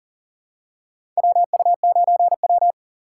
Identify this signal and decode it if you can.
Morse code, heard as a keyed tone, about 40 wpm, 700 Hz. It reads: WU9W
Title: USA Call Signs - 6x Character Spacing - 40wpm